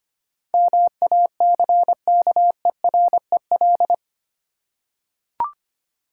Morse code, heard as a keyed tone, about 25 wpm, 700 Hz. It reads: MACKEREL
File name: practice / 25wpm / Easy Words with Hints - 25wpm